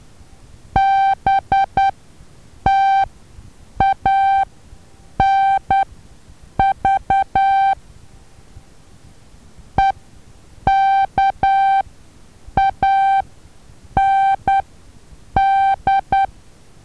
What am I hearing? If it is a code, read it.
Morse code, heard as a keyed tone, about 10 wpm, spaced as for 7 wpm, 780 Hz. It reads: BTANV EKAND
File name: Taviro HA6VA B